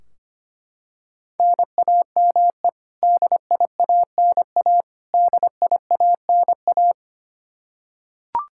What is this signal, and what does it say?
Morse code, heard as a keyed tone, about 25 wpm, 700 Hz. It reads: NAME DIANA DIANA
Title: Practice QSO - Exchange 1 - Beginning - 25wpm